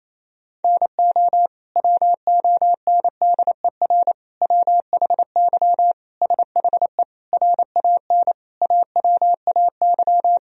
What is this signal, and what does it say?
Morse code, heard as a keyed tone, about 28 wpm, 700 Hz. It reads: NO WONDER WHY SHE RAN AWAY
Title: Sentences from Top 1000 Words - Review Part 3 - 28wpm